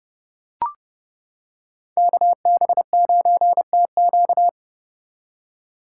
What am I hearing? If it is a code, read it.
Morse code, heard as a keyed tone, about 30 wpm, 700 Hz. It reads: KB9TQ